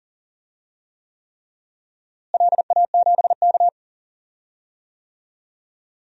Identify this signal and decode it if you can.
Morse code, heard as a keyed tone, about 40 wpm, 700 Hz. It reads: LA7K
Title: DXCC-100 Call Signs - 4 Characters - Repeated 3x - 40wpm